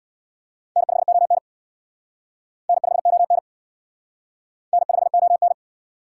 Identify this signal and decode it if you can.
Morse code, heard as a keyed tone, about 59 wpm, 700 Hz. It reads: N5ON N5ON N5ON
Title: Speed Racing - USA Call Signs 50wpm